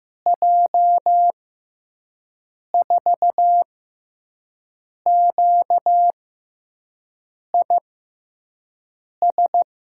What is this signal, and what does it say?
Morse code, heard as a keyed tone, about 15 wpm, spaced as for 5 wpm, 700 Hz. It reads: J4QIS
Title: USA Call Signs - 6 Characters - 6x Character Spacing - 15wpm